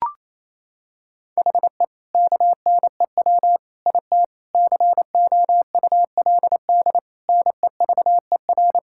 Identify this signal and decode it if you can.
Morse code, heard as a keyed tone, about 28 wpm, 700 Hz. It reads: HE KNEW IT COULD NEVER
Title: Sentences from Top 1000 Words - Review Part 6 - 28wpm